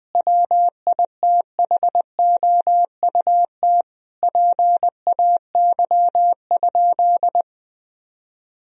Morse code, heard as a keyed tone, about 21 wpm, 700 Hz. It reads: WITHOUT PAY?
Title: Sentences from Top 1000 Words - Review Part 3 - 20wpm